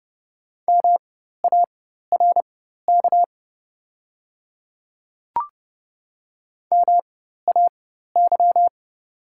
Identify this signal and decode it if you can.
Morse code, heard as a keyed tone, about 29 wpm, spaced as for 14 wpm, 700 Hz. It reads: MARK MAY
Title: 4x Character Spacing - Top 300 Words - 30wpm